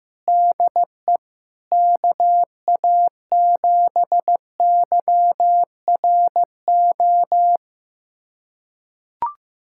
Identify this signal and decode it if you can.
Morse code, heard as a keyed tone, about 15 wpm, 700 Hz. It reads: DE KA7YRO
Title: USA Callsign DE Callsign - 6 Characters - 15wpm